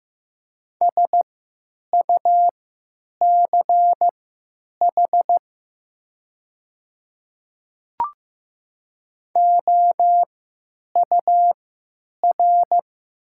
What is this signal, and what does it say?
Morse code, heard as a keyed tone, about 15 wpm, spaced as for 9 wpm, 700 Hz. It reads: SUCH OUR
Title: 3x Character Spacing - Top 300 Words - 15wpm